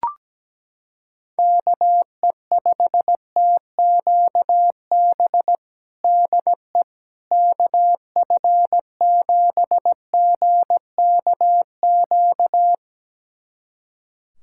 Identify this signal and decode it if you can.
Morse code, heard as a keyed tone, about 17 wpm, 700 Hz. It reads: KE5TQB DE KF7GKQ
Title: USA Callsign DE Callsign - 6 Characters - 17wpm